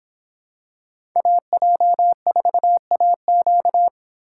Morse code, heard as a keyed tone, about 24 wpm, 700 Hz. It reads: AJ4AQ